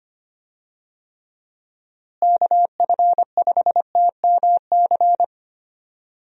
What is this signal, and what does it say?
Morse code, heard as a keyed tone, about 25 wpm, 700 Hz. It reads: KF5TMC
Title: USA Call Signs - 6 Characters - Repeated 3x - 25wpm